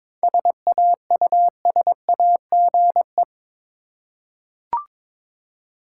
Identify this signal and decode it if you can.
Morse code, heard as a keyed tone, about 22 wpm, 700 Hz. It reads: SAUSAGE
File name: practice / 22wpm / Difficult Words with Hints - 22wpm